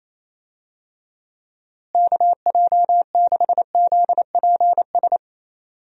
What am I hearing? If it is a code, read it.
Morse code, heard as a keyed tone, about 28 wpm, 700 Hz. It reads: KJ6ZPS
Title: USA Call Signs - 6 Characters - Repeated 2x - 28wpm